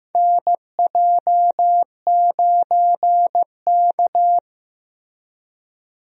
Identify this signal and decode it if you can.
Morse code, heard as a keyed tone, about 15 wpm, 700 Hz. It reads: NJ9K
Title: USA Call Signs - 4 Characters - Repeated 2x - 15wpm